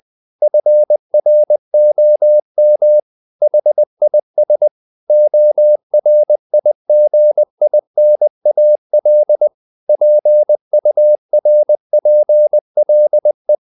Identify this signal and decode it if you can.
Morse code, heard as a keyed tone, about 20 wpm, 600 Hz. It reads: FROM HIS ORIGINAL PURPLE
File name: the old man and the sea part 2 20wpm